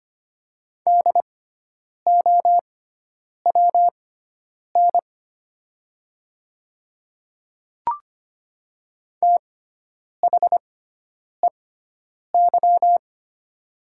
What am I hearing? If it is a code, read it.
Morse code, heard as a keyed tone, about 25 wpm, spaced as for 9 wpm, 700 Hz. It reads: DOWN THEY
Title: 6x Character Spacing - Top 100 Words - 25wpm